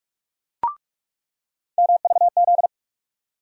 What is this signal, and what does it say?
Morse code, heard as a keyed tone, about 42 wpm, 700 Hz. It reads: MVZ
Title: USA Call Sign Postfixes - 3 Letters 45wpm